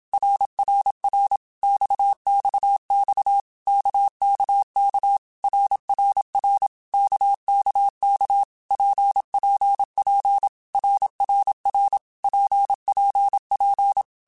Morse code, heard as a keyed tone, about 27 wpm, 800 Hz. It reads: RRR XXX KKK RRR KKK PPP RRR PPP
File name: Basic Letter Groups By7Groups / RKPX / RKPX@25